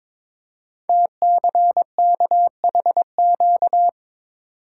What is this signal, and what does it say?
Morse code, heard as a keyed tone, about 22 wpm, 700 Hz. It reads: TCKHQ
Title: Code Groups - 5 Letters - 22wpm